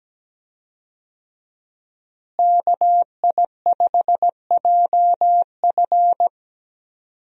Morse code, heard as a keyed tone, about 17 wpm, 700 Hz. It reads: KI5JF
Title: USA Call Signs - 5 Characters - Repeated 3x - 17wpm